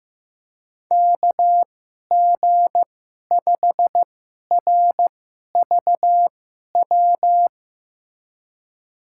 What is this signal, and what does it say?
Morse code, heard as a keyed tone, about 15 wpm, spaced as for 11 wpm, 700 Hz. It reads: KG5RVW
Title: USA Call Signs - 6 Characters - 2x Character Spacing - 15wpm